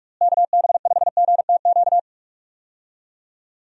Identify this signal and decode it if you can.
Morse code, heard as a keyed tone, about 46 wpm, 700 Hz. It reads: KB5GTQ
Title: USA Call Signs - 6 Characters - Repeated 2x - 45wpm